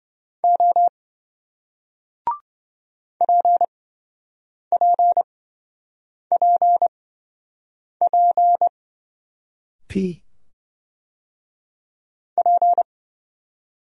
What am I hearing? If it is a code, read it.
Morse code, heard as a keyed tone, about 26 wpm, spaced as for 7 wpm, 700 Hz. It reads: O PPPP P